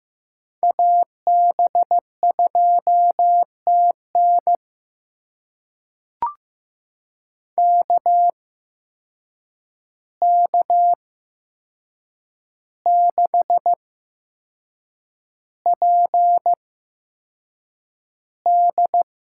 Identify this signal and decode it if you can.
Morse code, heard as a keyed tone, about 15 wpm, spaced as for 6 wpm, 700 Hz. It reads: AB2TN KK6PD
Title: USA Call Signs - 5 Characters - 8x Character Spacing - 15wpm